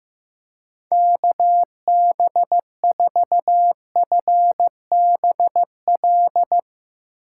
Morse code, heard as a keyed tone, about 15 wpm, 700 Hz. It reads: KB4FBL
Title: USA Call Signs - 8x Character Spacing - 15wpm